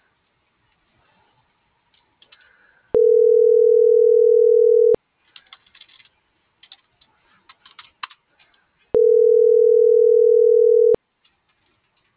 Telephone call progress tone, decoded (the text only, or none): ringback tone